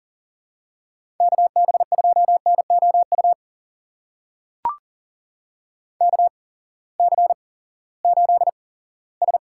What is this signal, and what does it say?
Morse code, heard as a keyed tone, about 38 wpm, spaced as for 20 wpm, 700 Hz. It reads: KB2NOU KC8S
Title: USA Call Signs - 6 Characters - 8x Character Spacing - 40wpm